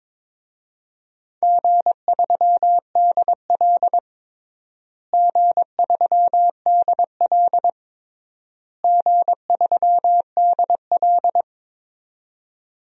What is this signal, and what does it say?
Morse code, heard as a keyed tone, about 22 wpm, 700 Hz. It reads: G3DL G3DL G3DL
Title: DXCC-100 Call Signs - 4 Characters - Repeated 3x - 22wpm